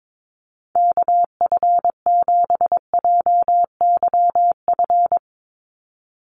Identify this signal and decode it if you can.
Morse code, heard as a keyed tone, about 22 wpm, 700 Hz. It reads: KF7JYF
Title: USA Call Signs 22wpm